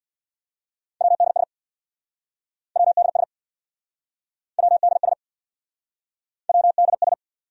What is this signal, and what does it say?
Morse code, heard as a keyed tone, about 68 wpm, 700 Hz. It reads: WDS WDS WDS WDS